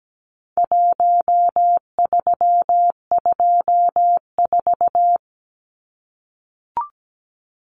Morse code, heard as a keyed tone, about 17 wpm, 700 Hz. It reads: 1324
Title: Numbers - Contest Style - 4-Digit 17wpm